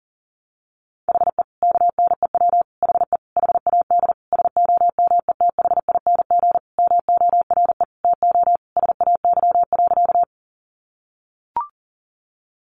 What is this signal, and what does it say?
Morse code, heard as a keyed tone, about 40 wpm, 700 Hz. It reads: HE KNEW HE HAD SOMETHING MORE TO SAY.
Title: Sentences from Top 600 Words 40wpm